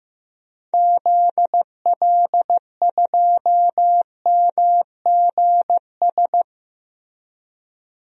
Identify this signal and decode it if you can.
Morse code, heard as a keyed tone, about 15 wpm, 700 Hz. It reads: ZL2MGS
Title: DXCC-100 Call Signs - 6 Characters - Repeated 2x - 15wpm